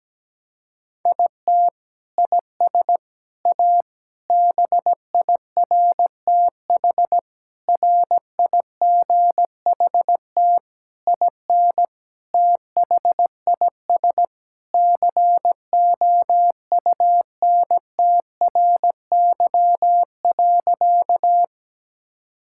Morse code, heard as a keyed tone, about 17 wpm, 700 Hz. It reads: IT IS A BIRTH RIGHT IN THIS COUNTRY.
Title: Sentences from Top 1000 Words - Review Part 5 - 17wpm